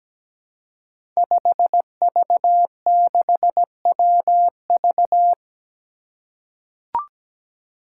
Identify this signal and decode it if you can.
Morse code, heard as a keyed tone, about 17 wpm, 700 Hz. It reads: 5V6WV
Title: Five Challenging Letter-Number Combinations 17wpm